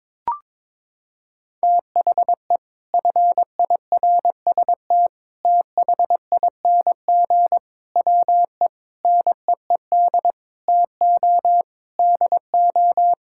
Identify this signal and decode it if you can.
Morse code, heard as a keyed tone, about 22 wpm, 700 Hz. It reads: THE FIRST THING WE NEED TO DO
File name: practice / 22wpm / Sets of 8 Words - 22wpm